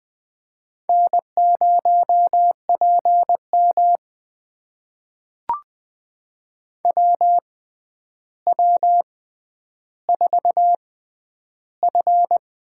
Morse code, heard as a keyed tone, about 20 wpm, spaced as for 10 wpm, 700 Hz. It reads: N0PM WW4F
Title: USA Call Signs - 4 Characters - 6x Character Spacing - 20wpm